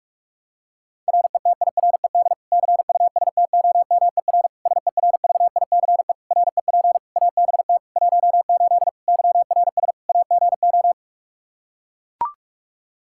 Morse code, heard as a keyed tone, about 42 wpm, 700 Hz. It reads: RETIRED CUSTOMER SERVICE REP ABT 18 YRS AGO
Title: Practice QSO - Exchange 3 - More Info - 45wpm